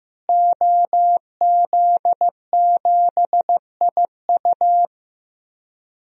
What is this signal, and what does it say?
Morse code, heard as a keyed tone, about 15 wpm, 700 Hz. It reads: OZ7IU
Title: DXCC-100 Call Signs - 5 Characters - Repeated 2x - 15wpm